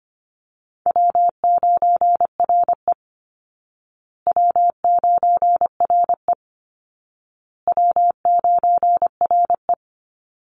Morse code, heard as a keyed tone, about 25 wpm, 700 Hz. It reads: W9RE W9RE W9RE
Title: Call Signs Repeated 2x - 25wpm